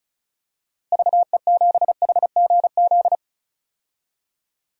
Morse code, heard as a keyed tone, about 36 wpm, 700 Hz. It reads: VE7HGZ
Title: DXCC-100 Call Signs - 6 Characters - Repeated 2x - 35wpm